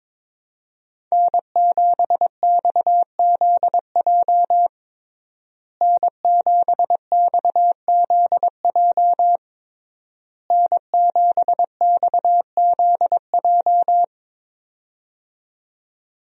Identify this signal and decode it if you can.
Morse code, heard as a keyed tone, about 22 wpm, 700 Hz. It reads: N7XZJ N7XZJ N7XZJ
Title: USA Call Signs - 5 Characters - Repeated 3x - 22wpm